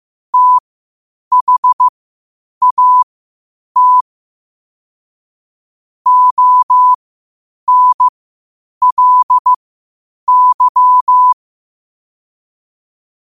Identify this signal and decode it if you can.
Morse code, heard as a keyed tone, about 15 wpm, spaced as for 8 wpm, 1000 Hz. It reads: THAT ONLY
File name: qotdfiles / 8wpm